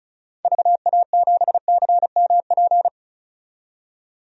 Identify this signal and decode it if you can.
Morse code, heard as a keyed tone, about 34 wpm, 700 Hz. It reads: VA7CMP
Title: DXCC-100 Call Signs - 6 Characters - 35wpm